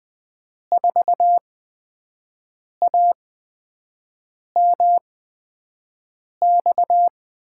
Morse code, heard as a keyed tone, about 20 wpm, spaced as for 5 wpm, 700 Hz. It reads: 4AMX
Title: USA Call Signs - 5 Characters - 8x Character Spacing - 20wpm